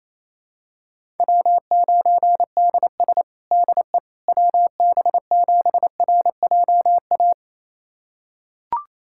Morse code, heard as a keyed tone, about 28 wpm, 700 Hz. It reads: W9DS DE WB7RJA
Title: USA Callsign DE Callsign 28wpm